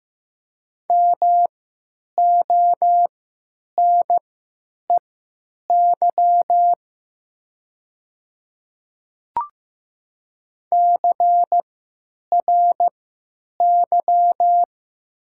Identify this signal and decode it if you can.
Morse code, heard as a keyed tone, about 15 wpm, spaced as for 8 wpm, 700 Hz. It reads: MONEY CRY
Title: 3x Character Spacing - Top 500 Words - 15wpm